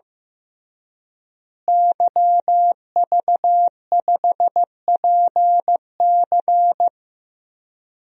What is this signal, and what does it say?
Morse code, heard as a keyed tone, about 15 wpm, 700 Hz. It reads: YV5PC
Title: DXCC-100 Call Signs - 5 Characters - Repeated 3x - 15wpm